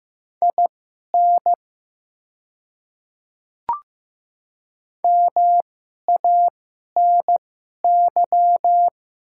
Morse code, heard as a keyed tone, about 15 wpm, spaced as for 11 wpm, 700 Hz. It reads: IN MANY
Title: 2x Character Spacing - Top 200 Words - 15wpm